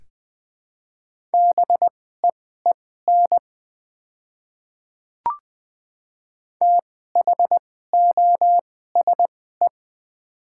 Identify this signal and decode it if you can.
Morse code, heard as a keyed tone, about 20 wpm, spaced as for 15 wpm, 700 Hz. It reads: BEEN THOSE
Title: 2x Character Spacing - Top 300 Words - 20wpm